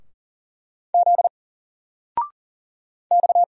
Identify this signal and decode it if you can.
Morse code, heard as a keyed tone, about 40 wpm, 700 Hz. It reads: Z X
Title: Single Character (Letter:Number:Punctuation:Prosign) 40wpm